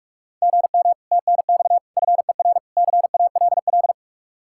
Morse code, heard as a keyed tone, about 56 wpm, 700 Hz. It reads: GM TNX FER CALL